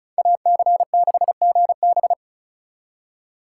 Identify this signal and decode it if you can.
Morse code, heard as a keyed tone, about 34 wpm, 700 Hz. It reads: AC6GB
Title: USA Call Signs - 5 Characters - 2x Character Spacing - 35wpm